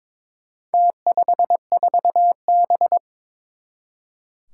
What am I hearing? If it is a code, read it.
Morse code, heard as a keyed tone, about 23 wpm, 700 Hz. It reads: T54B
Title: Four Challenging Letter-Number Combinations 22wpm